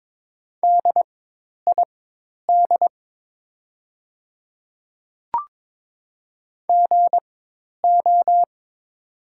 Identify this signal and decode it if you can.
Morse code, heard as a keyed tone, about 23 wpm, spaced as for 10 wpm, 700 Hz. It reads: DID GO
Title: 4x Character Spacing - Top 300 Words - 22wpm